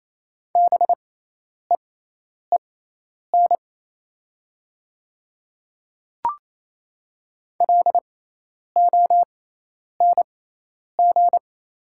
Morse code, heard as a keyed tone, about 28 wpm, spaced as for 10 wpm, 700 Hz. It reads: BEEN LONG
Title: 6x Character Spacing - Top 200 Words - 28wpm